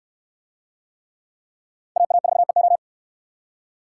Morse code, heard as a keyed tone, about 63 wpm, 700 Hz. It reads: AA4EO